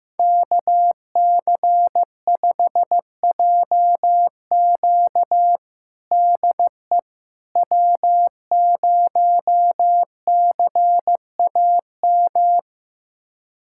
Morse code, heard as a keyed tone, about 16 wpm, 700 Hz. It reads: KC5JQ DE W0CAM
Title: USA Callsign DE Callsign - 5 Characters - 15wpm